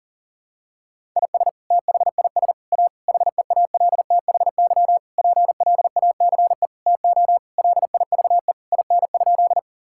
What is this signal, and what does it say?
Morse code, heard as a keyed tone, about 40 wpm, 700 Hz. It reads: IS THIS A HEALTHY PLACE TO LIVE IN?
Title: Sentences from Top 1000 Words - Review Part 5 - 40wpm